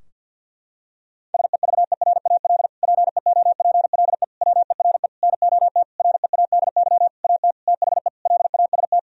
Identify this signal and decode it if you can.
Morse code, heard as a keyed tone, about 52 wpm, 700 Hz. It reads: SEVERAL PEOPLE WERE NOT READY AT THE LAST
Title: Sentences from Top 1000 Words - Review Part 2 - 50wpm